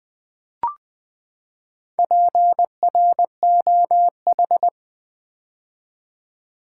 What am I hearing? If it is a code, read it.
Morse code, heard as a keyed tone, about 20 wpm, 700 Hz. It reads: PROH
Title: Build A Word - 4 Letters At A Time - 8-Letter Words - 20wpm